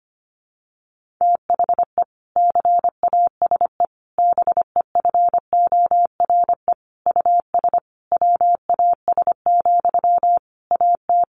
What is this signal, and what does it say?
Morse code, heard as a keyed tone, about 25 wpm, 700 Hz. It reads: THE CASE BEFORE US WAS, AT
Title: Sentences from Top 1400 Words 25wpm